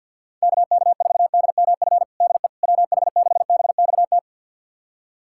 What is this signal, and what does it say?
Morse code, heard as a keyed tone, about 47 wpm, 700 Hz. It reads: KK4DMF DE WH6BXT
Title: USA Callsign DE Callsign 50wpm